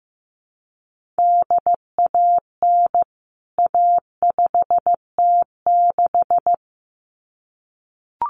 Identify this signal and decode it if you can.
Morse code, heard as a keyed tone, about 15 wpm, 700 Hz. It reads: DAN A5T6
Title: CWT Exchange 15wpm